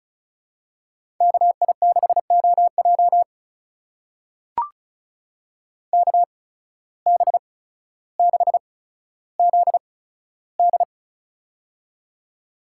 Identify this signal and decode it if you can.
Morse code, heard as a keyed tone, about 36 wpm, spaced as for 15 wpm, 700 Hz. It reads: KI6OJ KB6ZD
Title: USA Call Signs - 5 Characters - 8x Character Spacing - 35wpm